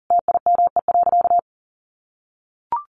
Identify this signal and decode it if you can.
Morse code, heard as a keyed tone, about 40 wpm, 700 Hz. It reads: TIME.
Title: Sentences from Top 1500 Words 40wpm